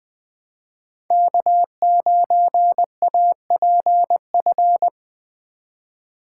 Morse code, heard as a keyed tone, about 20 wpm, 700 Hz. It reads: K9APF